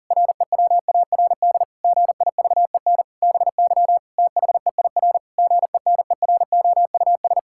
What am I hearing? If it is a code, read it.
Morse code, heard as a keyed tone, about 40 wpm, 700 Hz. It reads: REWARD GIVEN BY THEIR GENEROUS
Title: Pangram Challenge v2 - 40wpm